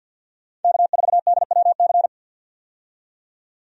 Morse code, heard as a keyed tone, about 50 wpm, 700 Hz. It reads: K4DWC
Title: USA Call Signs - 5 Characters - 50wpm